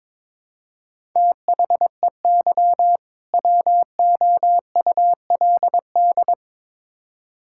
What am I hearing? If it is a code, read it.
Morse code, heard as a keyed tone, about 22 wpm, 700 Hz. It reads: THEY WOULD
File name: Build A Phrase - 2-Word Phrases - 22wpm